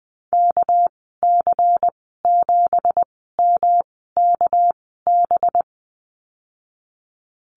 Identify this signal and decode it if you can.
Morse code, heard as a keyed tone, about 20 wpm, spaced as for 14 wpm, 700 Hz. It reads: KC7MKB
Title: 213 Lesson 34g - Number 8 - Call Signs 20wpm